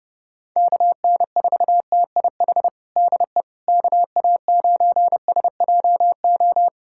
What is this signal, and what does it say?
Morse code, heard as a keyed tone, about 32 wpm, 700 Hz. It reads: KN4TIH DE KA9SJO
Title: USA Callsign DE Callsign - 6 Characters - 30wpm